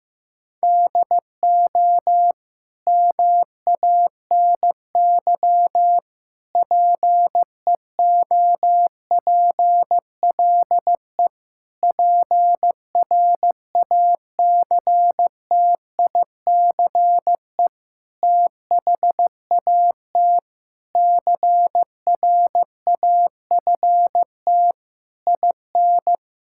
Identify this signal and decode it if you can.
Morse code, heard as a keyed tone, about 15 wpm, 700 Hz. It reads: DO MANY PEOPLE PRACTICE THAT CRAFT IN
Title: Sentences from Top 1000 Words - Review Part 6 - 15wpm